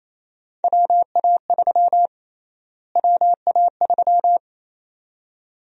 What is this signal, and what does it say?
Morse code, heard as a keyed tone, about 28 wpm, 700 Hz. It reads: WA3 WA3
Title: USA Call Sign Prefixes - Repeated 2x - 28wpm